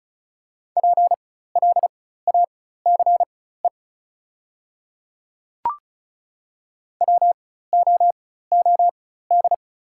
Morse code, heard as a keyed tone, about 36 wpm, spaced as for 16 wpm, 690 Hz. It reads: PLACE WOOD
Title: 4x Character Spacing - Top 300 Words - 35wpm